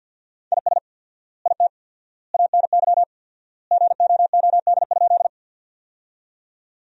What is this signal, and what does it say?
Morse code, heard as a keyed tone, about 50 wpm, 700 Hz. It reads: IS IT ANY GOOD?
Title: 4x Word-Spacing - Sentences from Top 200 Words - 50wpm